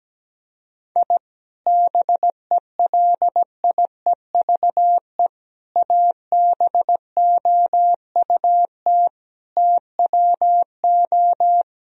Morse code, heard as a keyed tone, about 17 wpm, 700 Hz. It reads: I BELIEVE ABOUT TWO
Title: Sentences from Top 1000 Words - Review Part 3 - 17wpm